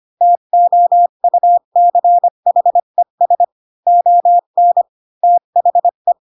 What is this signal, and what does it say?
Morse code, heard as a keyed tone, about 25 wpm, 700 Hz. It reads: TOUCHES ON THE
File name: Pinocchio / Pinocchio ch3 20 25 ik1hge